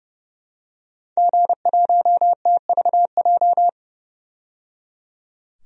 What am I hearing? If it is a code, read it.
Morse code, heard as a keyed tone, about 30 wpm, 700 Hz. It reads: G1TVJ
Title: DXCC-100 Call Signs - 5 Characters - 30wpm